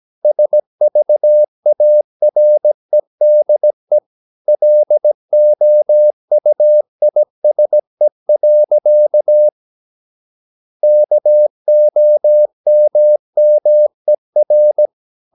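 Morse code, heard as a keyed tone, about 17 wpm, 600 Hz. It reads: SVAREDE LOUISE. KOMMER
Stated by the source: Slow (QRS) Morse practice at 17 wpm, from CW med Gnister.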